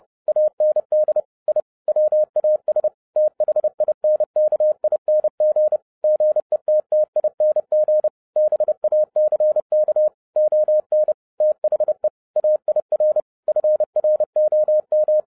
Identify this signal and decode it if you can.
Morse code, heard as a keyed tone, about 30 wpm, 610 Hz. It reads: AND I WAS THINKING GETTING BACK ON THE AIR FROM
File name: 2012-01-mp3 / n11-I DIDNT MISS THE SOLAR CYCLE 24-30wpm